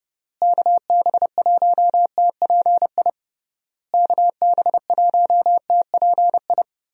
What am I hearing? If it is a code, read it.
Morse code, heard as a keyed tone, about 30 wpm, 700 Hz. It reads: KB1TPI KB1TPI
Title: USA Call Signs - 6 Characters - Repeated 3x - 30wpm